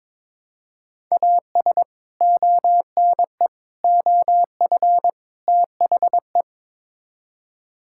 Morse code, heard as a keyed tone, about 22 wpm, 700 Hz. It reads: AS ONE OF THE
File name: practice / 22wpm / Build A Phrase - 5-Word Phrases - 22wpm